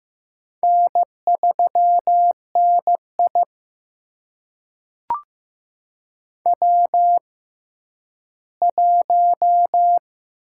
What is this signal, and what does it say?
Morse code, heard as a keyed tone, about 15 wpm, spaced as for 10 wpm, 700 Hz. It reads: N3NI W1
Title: USA Call Signs - 6x Character Spacing - 15wpm